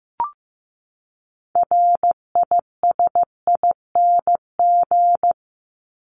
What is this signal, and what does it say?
Morse code, heard as a keyed tone, about 15 wpm, 700 Hz. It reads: RISING
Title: ING Suffix - Top 100 Words with Base Words 15wpm